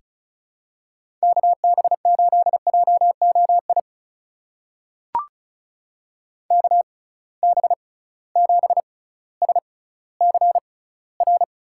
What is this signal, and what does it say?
Morse code, heard as a keyed tone, about 36 wpm, spaced as for 18 wpm, 700 Hz. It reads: KB8JOI KB7SCR
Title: USA Call Signs - 6x Character Spacing - 35wpm